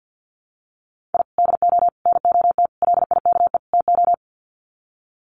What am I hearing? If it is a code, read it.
Morse code, heard as a keyed tone, about 50 wpm, 700 Hz. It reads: I DO NOT LIKE TO
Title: Sets of 5 Words 50wpm